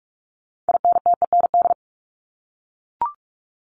Morse code, heard as a keyed tone, about 44 wpm, 700 Hz. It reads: INTEND
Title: Top 1501-1750 Words 45wpm